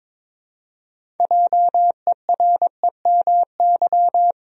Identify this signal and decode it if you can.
Morse code, heard as a keyed tone, about 22 wpm, 700 Hz. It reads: JEREMY
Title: USA Common Names - 22wpm